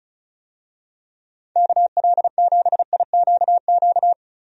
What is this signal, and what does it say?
Morse code, heard as a keyed tone, about 36 wpm, 690 Hz. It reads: KL7IQQ
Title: DXCC-100 Call Signs - 6 Characters - 35wpm